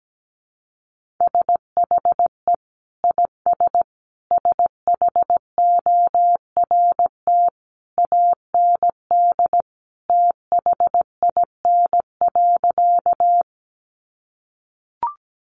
Morse code, heard as a keyed tone, about 17 wpm, 700 Hz. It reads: SHE IS SHORT AND THIN.
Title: Sentences from Top 1200 Words 17wpm